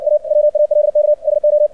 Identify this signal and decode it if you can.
Morse code, heard as a keyed tone, about 60 wpm, 600 Hz. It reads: A WALK SO
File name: archived / CRSnet